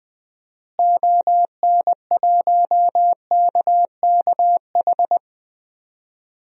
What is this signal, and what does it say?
Morse code, heard as a keyed tone, about 20 wpm, 700 Hz. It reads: ON1KKH
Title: DXCC-100 Call Signs - 6 Characters - Repeated 2x - 20wpm